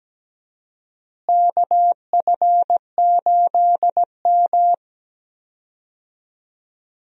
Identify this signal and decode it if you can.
Morse code, heard as a keyed tone, about 17 wpm, 700 Hz. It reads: KF8M
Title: USA Call Signs - 4 Characters - Repeated 2x - 17wpm